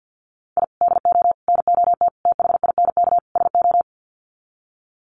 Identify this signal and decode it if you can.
Morse code, heard as a keyed tone, about 48 wpm, 700 Hz. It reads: I DO NOT THINK SO
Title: Sets of 5 Words 50wpm